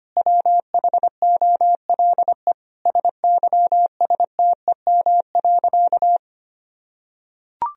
Morse code, heard as a keyed tone, about 25 wpm, 700 Hz. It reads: WHOLE SYSTEM.
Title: Sentences from Top 1000 Words - Review Part 3 - 25wpm